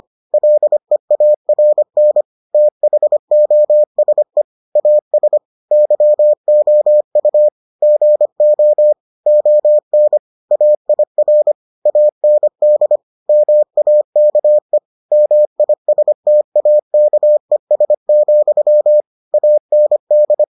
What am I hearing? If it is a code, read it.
Morse code, heard as a keyed tone, about 25 wpm, 600 Hz. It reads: LEARN THOSE AS YOU GO ON AIR AND MAKE MISTAKES, AND